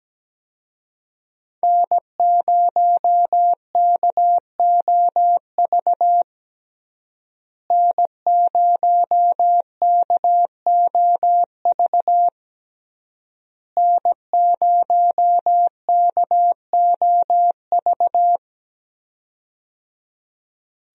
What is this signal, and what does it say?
Morse code, heard as a keyed tone, about 17 wpm, 700 Hz. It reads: N0KOV N0KOV N0KOV
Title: USA Call Signs - 5 Characters - Repeated 3x - 17wpm